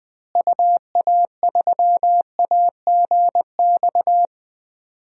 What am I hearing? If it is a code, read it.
Morse code, heard as a keyed tone, about 20 wpm, 700 Hz. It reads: UA3AGX
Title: DXCC-100 Call Signs - 6 Characters - Repeated 3x - 20wpm